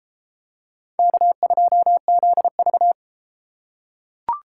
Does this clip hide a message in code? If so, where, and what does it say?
Yes: Morse code from 1.0 to 2.9 s, heard as a keyed tone, about 32 wpm, 700 Hz. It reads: K2ZV